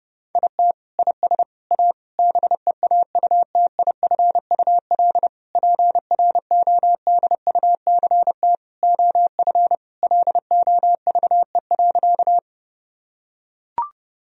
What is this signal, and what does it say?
Morse code, heard as a keyed tone, about 30 wpm, 700 Hz. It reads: IT IS A BEAUTIFUL PRODUCT OF LOVE.
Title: Sentences from Top 1000 Words - Review Part 2 - 30wpm